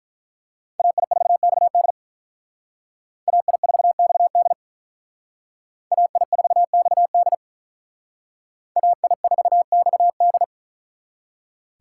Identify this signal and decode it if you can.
Morse code, heard as a keyed tone, about 43 wpm, 700 Hz. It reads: AI4XD AI4XD AI4XD AI4XD
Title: Speed Racing - USA Call Signs 35wpm